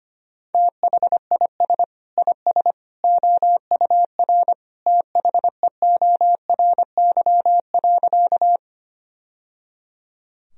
Morse code, heard as a keyed tone, about 25 wpm, 700 Hz. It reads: THIS IS OUR THEORY.
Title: Sentences from Top 1000 Words - Review Part 3 - 25wpm